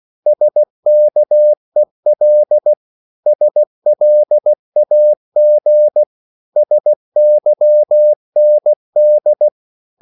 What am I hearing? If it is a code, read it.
Morse code, heard as a keyed tone, about 16 wpm, 600 Hz. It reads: SKEL SLAG SYND
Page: Korte danske ord 16wpm